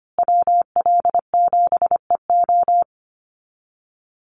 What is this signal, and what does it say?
Morse code, heard as a keyed tone, about 25 wpm, 700 Hz. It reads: WL7EO
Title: USA Call Signs Repeated 3x 25wpm